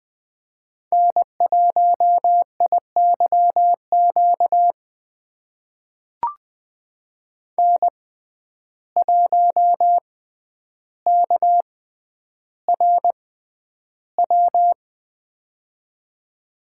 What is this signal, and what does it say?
Morse code, heard as a keyed tone, about 20 wpm, spaced as for 10 wpm, 700 Hz. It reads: N1IYQ N1KRW
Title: USA Call Signs - 6x Character Spacing - 20wpm